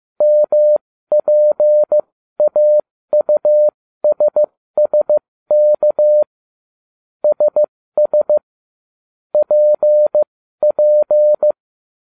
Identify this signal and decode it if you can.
Morse code, heard as a keyed tone, about 15 wpm, spaced as for 13 wpm, 600 Hz. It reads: MPAUSSK SS PP
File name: cw / ubacw07 1